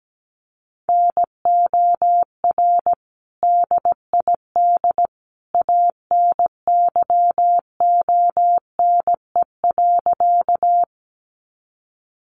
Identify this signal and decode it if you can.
Morse code, heard as a keyed tone, about 17 wpm, 700 Hz. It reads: NOR DID ANYONE.
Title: Sentences from Top 1500 Words 17wpm